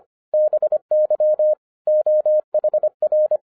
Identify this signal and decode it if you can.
Morse code, heard as a keyed tone, about 26 wpm, 610 Hz. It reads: BY OHR
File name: 2011-07-mp3 / n24 QRP OTAKU 25wpm